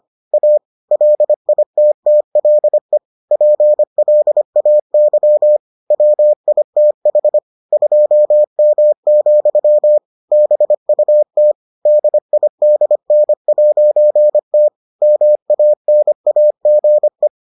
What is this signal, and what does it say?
Morse code, heard as a keyed tone, about 25 wpm, 600 Hz. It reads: A LITTLE PLAY WITH 2M, BUT DIDN'T MANAGE